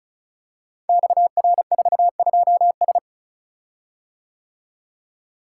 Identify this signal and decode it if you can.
Morse code, heard as a keyed tone, about 36 wpm, 700 Hz. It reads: XR42S
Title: Code Groups - 5 Letter:Numbers - 35wpm